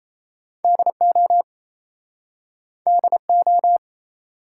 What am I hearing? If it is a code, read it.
Morse code, heard as a keyed tone, about 38 wpm, 700 Hz. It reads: DO DO